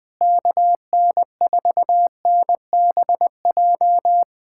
Morse code, heard as a keyed tone, about 20 wpm, 700 Hz. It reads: KN4NBJ
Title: POTA QSO Practice - 20wpm